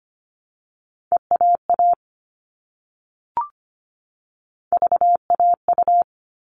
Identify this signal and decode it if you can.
Morse code, heard as a keyed tone, about 25 wpm, spaced as for 10 wpm, 700 Hz. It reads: EAAVAU
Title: All Cut Numbers - 3-Digit 25wpm